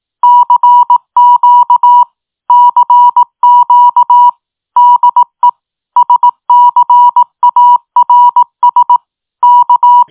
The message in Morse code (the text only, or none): CQ CQ DE SCARS K